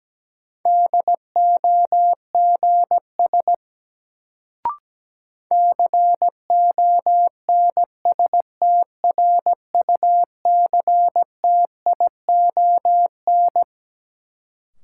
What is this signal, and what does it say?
Morse code, heard as a keyed tone, about 17 wpm, 700 Hz. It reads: DOGS CONSTRUCTION
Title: Sentences from Top 1600 Words - New Word Drill - 17wpm